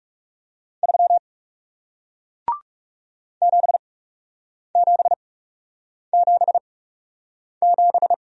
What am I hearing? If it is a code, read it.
Morse code, heard as a keyed tone, about 40 wpm, spaced as for 8 wpm, 700 Hz. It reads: 3 7777